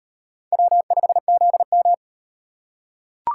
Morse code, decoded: W5ZM

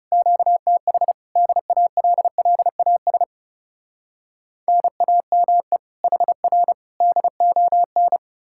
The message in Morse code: QTH DALLAS NAME HR DON

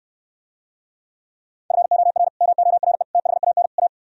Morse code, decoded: FOR MORE THAN A

